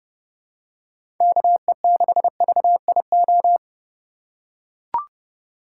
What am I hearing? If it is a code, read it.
Morse code, heard as a keyed tone, about 32 wpm, 700 Hz. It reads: KE6VIO
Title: USA Call Signs - 6 Characters - 2x Character Spacing - 30wpm